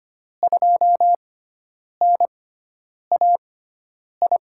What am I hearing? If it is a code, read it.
Morse code, heard as a keyed tone, about 25 wpm, spaced as for 9 wpm, 700 Hz. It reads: 2NAI
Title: USA Call Signs - 6 Characters - 6x Character Spacing - 25wpm